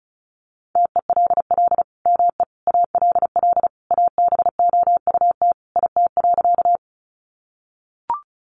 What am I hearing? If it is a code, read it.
Morse code, heard as a keyed tone, about 34 wpm, 700 Hz. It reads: TELL ME ALL ABOUT IT.
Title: Sentences from Top 200 Words 35wpm